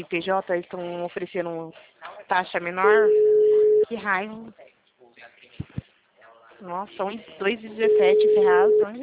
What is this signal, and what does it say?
Ringback tone (Europe, 425 Hz), with speech.